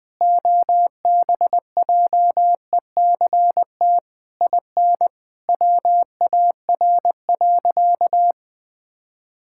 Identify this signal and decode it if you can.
Morse code, heard as a keyed tone, about 20 wpm, 700 Hz. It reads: OBJECT IN WAR.
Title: Sentences from Top 1000 Words - Review Part 6 - 20wpm